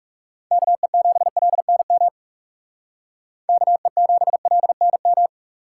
Morse code, heard as a keyed tone, about 38 wpm, 700 Hz. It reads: KE7LNM KE7LNM